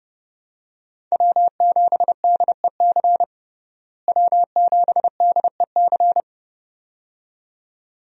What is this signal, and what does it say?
Morse code, heard as a keyed tone, about 32 wpm, 700 Hz. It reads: W7DEC W7DEC
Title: USA Call Signs - 5 Characters - Repeated 2x - 30wpm